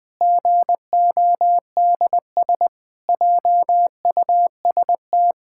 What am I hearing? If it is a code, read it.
Morse code, heard as a keyed tone, about 21 wpm, 700 Hz. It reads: GODS JUST
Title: Pangram Challenge v2 - 20wpm